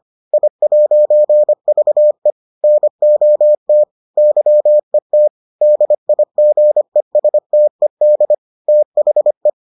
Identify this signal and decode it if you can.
Morse code, heard as a keyed tone, about 25 wpm, 600 Hz. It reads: I'VE NOT YET DIGESTED THE